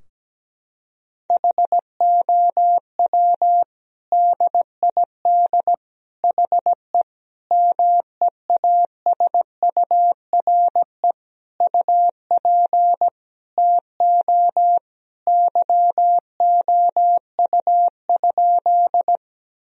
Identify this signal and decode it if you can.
Morse code, heard as a keyed tone, about 17 wpm, 700 Hz. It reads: HOW DID HE MEASURE UP TO YOU?
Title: Sentences from Top 1000 Words - Review Part 2 - 17wpm